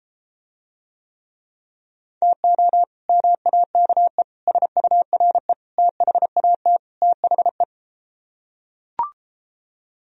Morse code, heard as a keyed tone, about 33 wpm, 700 Hz. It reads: TO MAKE SURE THAT THE